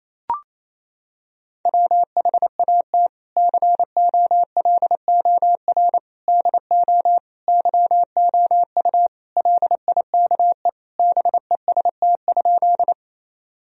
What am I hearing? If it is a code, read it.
Morse code, heard as a keyed tone, about 28 wpm, 700 Hz. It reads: WHAT COLOR DO YOU LIKE BEST?
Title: Sentences from Top 1000 Words - Review Part 2 - 28wpm